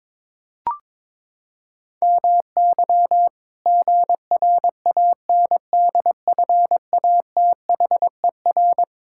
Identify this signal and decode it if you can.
Morse code, heard as a keyed tone, about 22 wpm, 700 Hz. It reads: MY GRANDFATHER
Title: Pangram Challenge v2 - 22wpm